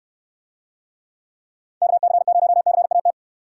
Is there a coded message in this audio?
Yes: Morse code from 1.8 to 3.1 s, heard as a keyed tone, about 69 wpm, 700 Hz. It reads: KC0CAT